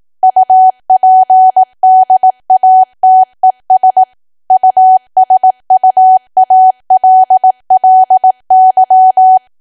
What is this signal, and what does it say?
Morse code, heard as a keyed tone, about 18 wpm, 750 Hz. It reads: UPDATES USUALLY
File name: cwa / W1AW-Level2 / 18WPM